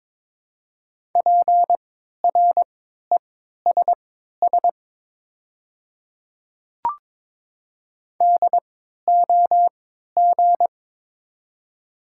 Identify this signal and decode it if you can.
Morse code, heard as a keyed tone, about 22 wpm, spaced as for 12 wpm, 700 Hz. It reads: PRESS DOG
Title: 3x Character Spacing - Top 500 Words - 22wpm